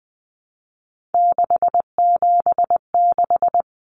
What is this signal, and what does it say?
Morse code, heard as a keyed tone, about 20 wpm, 700 Hz. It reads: 676
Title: Numbers - 3-Digit 20wpm